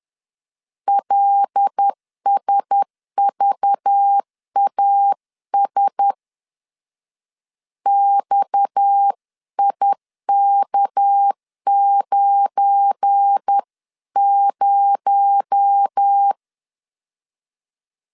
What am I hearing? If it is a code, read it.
Morse code, heard as a keyed tone, about 10 wpm, 800 Hz. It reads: LSVAS XIK90
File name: LEZIONI / 8 YJ45 / 8 YJ 10wpm 2651